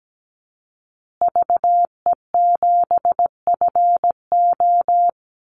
Morse code, heard as a keyed tone, about 17 wpm, 700 Hz. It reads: VE7FO
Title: Call Signs Repeated 3x - 17wpm